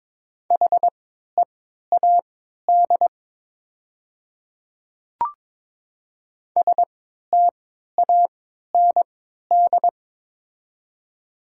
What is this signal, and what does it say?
Morse code, heard as a keyed tone, about 22 wpm, spaced as for 12 wpm, 700 Hz. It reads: HEAD STAND
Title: 3x Character Spacing - Top 500 Words - 22wpm